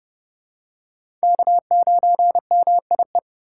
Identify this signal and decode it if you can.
Morse code, heard as a keyed tone, about 30 wpm, 700 Hz. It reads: K9MIE
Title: USA Call Signs - 5 Characters - 3x Character Spacing - 30wpm